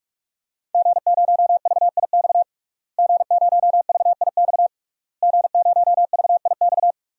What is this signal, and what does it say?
Morse code, heard as a keyed tone, about 46 wpm, 700 Hz. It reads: G0VIX G0VIX G0VIX
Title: DXCC-100 Call Signs - 5 Characters - Repeated 3x - 45wpm